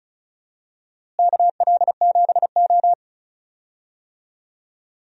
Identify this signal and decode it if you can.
Morse code, heard as a keyed tone, about 36 wpm, 700 Hz. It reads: KL7O
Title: DXCC-100 Call Signs - 4 Characters - Repeated 3x - 35wpm